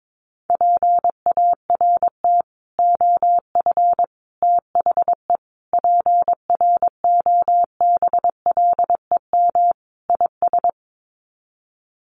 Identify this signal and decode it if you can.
Morse code, heard as a keyed tone, about 22 wpm, 700 Hz. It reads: PART OF THE PROBLEM IS
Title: Sets of 5 Words Encore2 22wpm